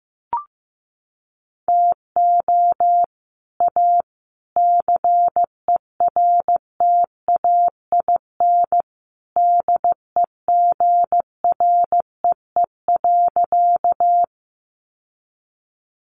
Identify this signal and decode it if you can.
Morse code, heard as a keyed tone, about 15 wpm, 700 Hz. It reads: TO A CERTAIN DEGREE.
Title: Sentences from Top 1500 Words 15wpm